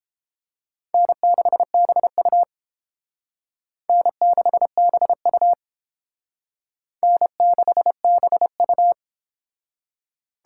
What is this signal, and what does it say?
Morse code, heard as a keyed tone, about 31 wpm, 700 Hz. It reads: N6BU N6BU N6BU